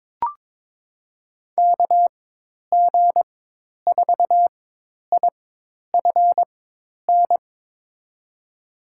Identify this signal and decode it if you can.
Morse code, heard as a keyed tone, about 22 wpm, spaced as for 10 wpm, 700 Hz. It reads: KG4IFN
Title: USA Call Signs - 6 Characters - 4x Character Spacing - 22wpm